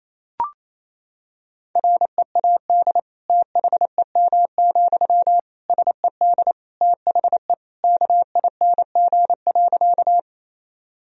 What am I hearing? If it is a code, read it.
Morse code, heard as a keyed tone, about 28 wpm, 700 Hz. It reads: READ THEM, SED THE KING.
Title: Sentences from Top 400 Words and Abbreviations 28wpm